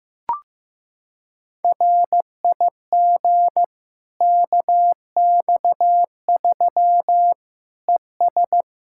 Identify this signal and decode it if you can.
Morse code, heard as a keyed tone, about 15 wpm, 700 Hz. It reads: RIG KX3 ES
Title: Practice QSO - Exchange 2 - Info - 15wpm